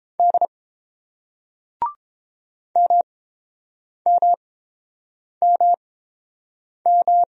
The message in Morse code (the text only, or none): D MMMM